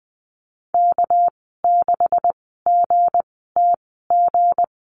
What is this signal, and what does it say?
Morse code, heard as a keyed tone, about 20 wpm, spaced as for 15 wpm, 700 Hz. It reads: K6GTG
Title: 231 Lesson 37g - Letter x - Call Signs 20wpm